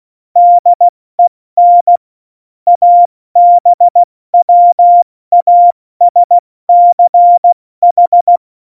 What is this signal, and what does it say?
Morse code, heard as a keyed tone, about 16 wpm, spaced as for 14 wpm, 700 Hz. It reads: DEN ABWASCH
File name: Morsekurs / Witze / Witz 02. Juli 2019, 70 F80